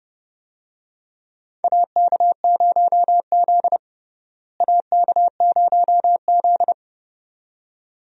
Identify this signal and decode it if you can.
Morse code, heard as a keyed tone, about 30 wpm, 700 Hz. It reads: AK0Z AK0Z
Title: USA Call Signs - 4 Characters - Repeated 2x - 30wpm